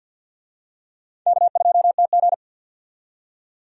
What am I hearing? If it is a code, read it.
Morse code, heard as a keyed tone, about 53 wpm, 700 Hz. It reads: K2TG
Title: USA Call Signs - 4 Characters - 50wpm